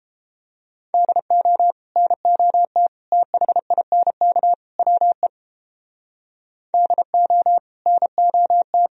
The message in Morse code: DO NOT THINK WE DO NOT

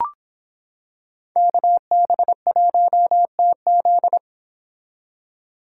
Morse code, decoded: KB1TZ

